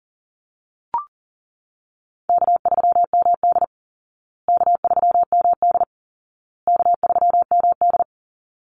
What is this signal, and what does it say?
Morse code, heard as a keyed tone, about 40 wpm, 700 Hz. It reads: K3MD K3MD K3MD
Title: Easy Call Signs Repeated 3x - 40wpm